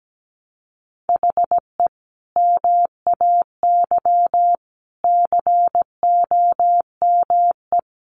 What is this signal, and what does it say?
Morse code, heard as a keyed tone, about 17 wpm, 700 Hz. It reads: HE MAY COME
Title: Sentences from Top 300 Words 17wpm